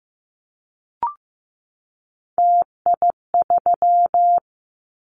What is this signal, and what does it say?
Morse code, heard as a keyed tone, about 15 wpm, 700 Hz. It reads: TI3
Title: Callsign Prefixes for Top 100 DXCC - 15wpm